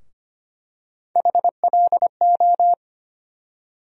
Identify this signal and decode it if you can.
Morse code, heard as a keyed tone, about 25 wpm, 700 Hz. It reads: HLO
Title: USA Call Sign Postfixes - 3 Letters 25wpm